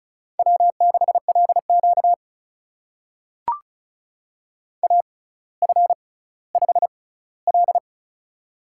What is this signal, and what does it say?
Morse code, heard as a keyed tone, about 36 wpm, spaced as for 18 wpm, 700 Hz. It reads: W6LQ AF5L
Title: USA Call Signs - 4 Characters - 6x Character Spacing - 35wpm